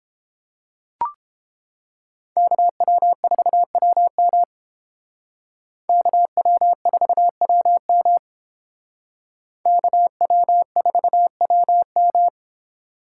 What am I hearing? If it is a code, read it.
Morse code, heard as a keyed tone, about 29 wpm, 700 Hz. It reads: KW4WM KW4WM KW4WM